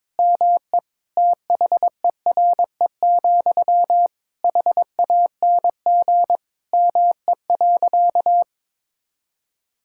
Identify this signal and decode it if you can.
Morse code, heard as a keyed tone, about 22 wpm, 700 Hz. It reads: ME THERE, HANG ME.
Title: Sentences from Top 1600 Words 22wpm